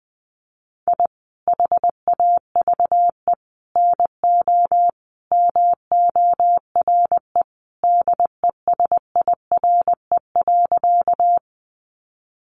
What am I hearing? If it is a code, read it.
Morse code, heard as a keyed tone, about 20 wpm, 700 Hz. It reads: I HAVE NO MORE DESIRE.
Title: Sentences from Top 1100 Words 20wpm